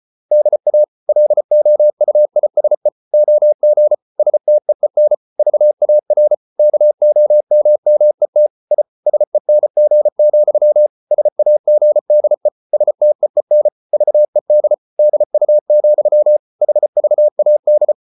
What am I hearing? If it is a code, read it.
Morse code, heard as a keyed tone, about 34 wpm, 600 Hz. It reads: DA LOUISE OG STEEN VAR KOMMET I SENG, SAGDE STEEN VED DU, HVAD
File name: Ferie QRQ 34wpm